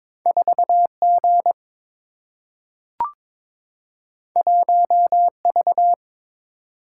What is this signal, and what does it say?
Morse code, heard as a keyed tone, about 22 wpm, 700 Hz. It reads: 4 G 1 V
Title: Single-Letter Single-Number 22wpm